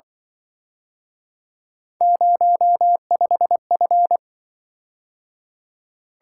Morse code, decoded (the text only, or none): 05F